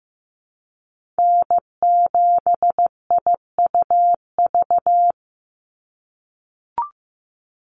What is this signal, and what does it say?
Morse code, heard as a keyed tone, about 15 wpm, 700 Hz. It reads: N7IUV
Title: USA Call Signs 15wpm